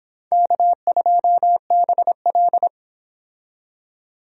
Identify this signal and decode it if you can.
Morse code, heard as a keyed tone, about 26 wpm, 700 Hz. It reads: K2BL